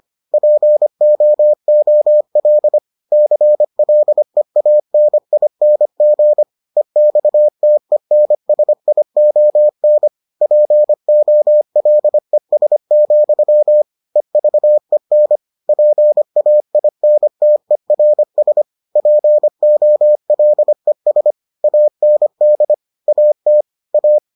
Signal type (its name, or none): Morse code